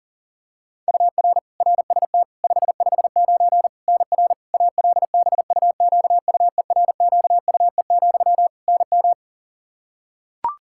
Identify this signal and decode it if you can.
Morse code, heard as a keyed tone, about 40 wpm, 700 Hz. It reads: UR RST 559 NR ALBUQUERQUE, NM
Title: Practice QSO - Exchange 1 - Beginning - 40wpm